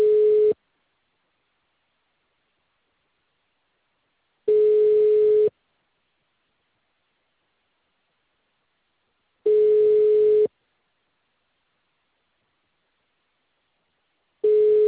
Ringback tone (Europe, 425 Hz).
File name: out- -2020- - - .47